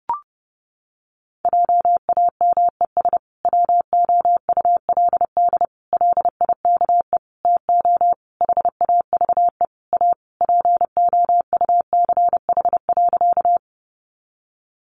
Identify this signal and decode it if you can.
Morse code, heard as a keyed tone, about 30 wpm, 700 Hz. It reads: JAMES WOULD LIKE TO HAVE A POUCH.
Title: Easy Phrases 30wpm